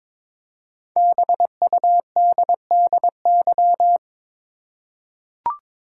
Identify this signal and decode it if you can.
Morse code, heard as a keyed tone, about 23 wpm, 700 Hz. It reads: BUDDY
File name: 5 Letter Words - First 3 Letters Given - 22wpm